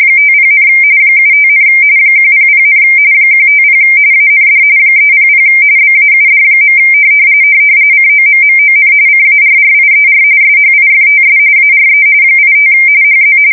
RTTY, about 45.45 baud. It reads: 50, 75, 100, 150 AND 300 BAUD. COMMON CARRIER SHIFTS ARE 85 HZ
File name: fsk170 / teletype-RTTY